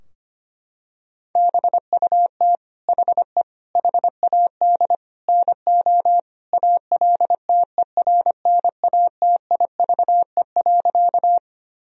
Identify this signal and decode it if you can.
Morse code, heard as a keyed tone, about 25 wpm, 700 Hz. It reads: BUT HE HAD NO ALTERNATIVE.
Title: Sentences from Top 1000 Words - Review Part 4 - 25wpm